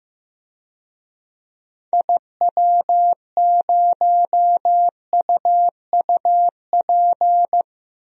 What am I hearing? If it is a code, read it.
Morse code, heard as a keyed tone, about 15 wpm, 700 Hz. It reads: IW0UUP
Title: DXCC-100 Call Signs - 6 Characters - 15wpm